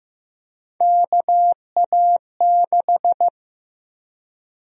Morse code, heard as a keyed tone, about 15 wpm, 700 Hz. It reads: KA6
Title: USA Call Sign Prefixes - Repeated 2x - 15wpm